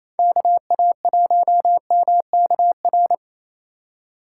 Morse code, heard as a keyed tone, about 33 wpm, 700 Hz. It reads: KA1MKR
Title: Speed Racing - USA 2x3 Call Signs 28wpm